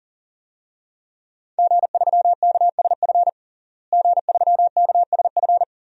Morse code, heard as a keyed tone, about 40 wpm, 700 Hz. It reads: G3KSF G3KSF
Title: DXCC-100 Call Signs - 5 Characters - Repeated 3x - 40wpm